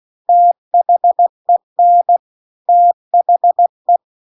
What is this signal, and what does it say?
Morse code, heard as a keyed tone, about 16 wpm, 700 Hz. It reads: THEN THE
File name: CW-The-Big-Twoheaded-River-2-16WPM